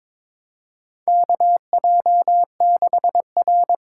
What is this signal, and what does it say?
Morse code, heard as a keyed tone, about 22 wpm, 700 Hz. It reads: KJ6R
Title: USA Call Signs - 6 Characters - 8x Character Spacing - 22wpm